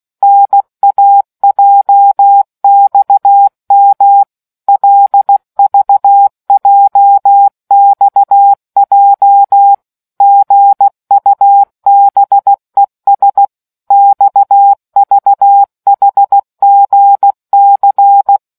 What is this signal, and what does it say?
Morse code, heard as a keyed tone, about 16 wpm, 800 Hz. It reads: NAJXM LVJXJ GUBES XVHGC